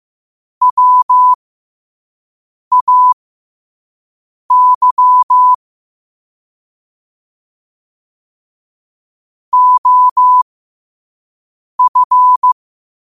Morse code, heard as a keyed tone, about 15 wpm, spaced as for 5 wpm, 1000 Hz. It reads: WAY OF
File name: qotdfiles / 5wpm